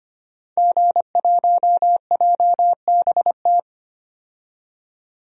Morse code, heard as a keyed tone, about 25 wpm, 700 Hz. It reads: G1JBT
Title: DXCC-100 Call Signs - 5 Characters - 25wpm